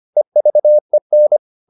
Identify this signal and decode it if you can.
Morse code, heard as a keyed tone, about 25 wpm, 600 Hz. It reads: EVEN
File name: the old man and the sea part 1 25wpm